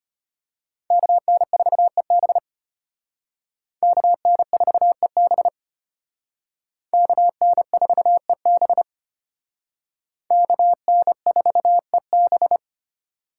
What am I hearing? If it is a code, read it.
Morse code, heard as a keyed tone, about 32 wpm, 700 Hz. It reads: KN4EB KN4EB KN4EB KN4EB